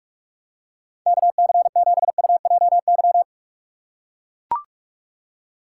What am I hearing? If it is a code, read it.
Morse code, heard as a keyed tone, about 44 wpm, 700 Hz. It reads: KC7UJY